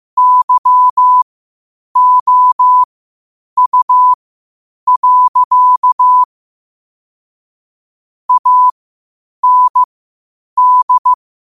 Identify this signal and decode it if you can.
Morse code, heard as a keyed tone, about 15 wpm, spaced as for 8 wpm, 1000 Hz. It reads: YOU. AND
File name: qotdfiles / 8wpm